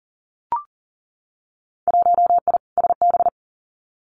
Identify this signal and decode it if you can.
Morse code, heard as a keyed tone, about 40 wpm, 700 Hz. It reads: 1I SB
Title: Winter Field Day Exchanges Encore 40wpm